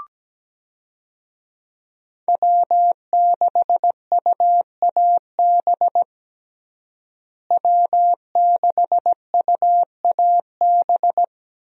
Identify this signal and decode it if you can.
Morse code, heard as a keyed tone, about 17 wpm, 700 Hz. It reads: W6UAB W6UAB
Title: USA Call Signs - 5 Characters - Repeated 2x - 17wpm